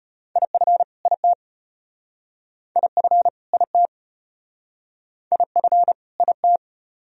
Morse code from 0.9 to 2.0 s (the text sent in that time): IT